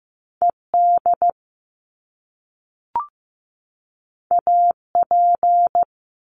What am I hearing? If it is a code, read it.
Morse code, heard as a keyed tone, about 15 wpm, 700 Hz. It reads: ED AP
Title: Top Two Letter Combinations in Words 15wpm